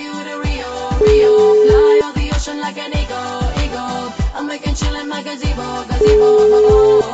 Ringback tone (Europe, 425 Hz), with noise.